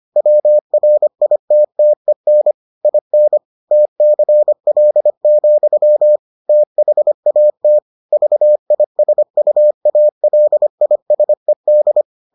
Morse code, heard as a keyed tone, about 25 wpm, 600 Hz. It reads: WRITTEN IN TCL, THAT VISUALISED